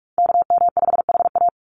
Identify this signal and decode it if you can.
Morse code, heard as a keyed tone, about 44 wpm, 700 Hz. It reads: KM5HA
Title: USA Call Signs 45wpm